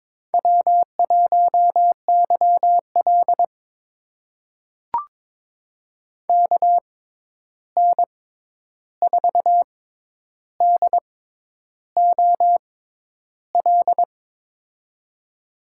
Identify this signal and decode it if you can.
Morse code, heard as a keyed tone, about 23 wpm, spaced as for 10 wpm, 700 Hz. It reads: W1YL KN4DOL
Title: USA Call Signs - 6x Character Spacing - 22wpm